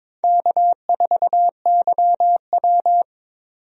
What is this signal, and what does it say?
Morse code, heard as a keyed tone, about 22 wpm, 700 Hz. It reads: K4YW
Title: USA Call Signs - 4 Characters - 22wpm